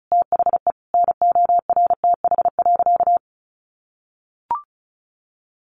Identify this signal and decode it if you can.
Morse code, heard as a keyed tone, about 34 wpm, 700 Hz. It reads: THE NORTH.
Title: Sentences from Top 1200 Words 35wpm